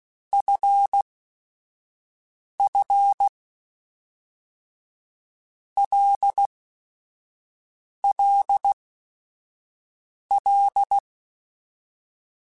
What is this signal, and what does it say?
Morse code, heard as a keyed tone, about 16 wpm, 800 Hz. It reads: F F L L L
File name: Basic Sets / Basic Letter Groups Collective7 / LFCYQ@5